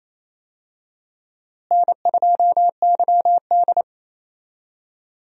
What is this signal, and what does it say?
Morse code, heard as a keyed tone, about 27 wpm, 700 Hz. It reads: N2YD
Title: USA Call Signs - 4 Characters - Repeated 3x - 28wpm